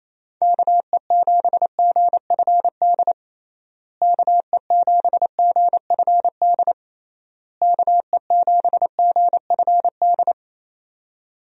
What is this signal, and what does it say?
Morse code, heard as a keyed tone, about 28 wpm, 700 Hz. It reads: KE7GFD KE7GFD KE7GFD
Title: USA Call Signs - 6 Characters - Repeated 3x - 28wpm